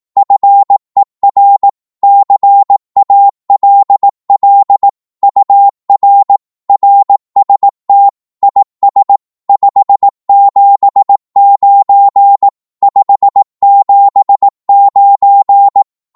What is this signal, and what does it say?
Morse code, heard as a keyed tone, about 18 wpm, 800 Hz. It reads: FER CALL UR RST IS 579 579
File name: Morse / qso-18wpm